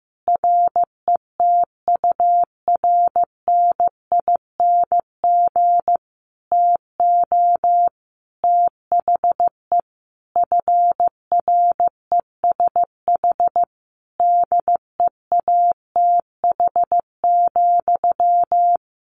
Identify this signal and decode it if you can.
Morse code, heard as a keyed tone, about 15 wpm, 700 Hz. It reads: RETURNING TO THE FRESH DEATH,
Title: A Sound Of Thunder 15wpm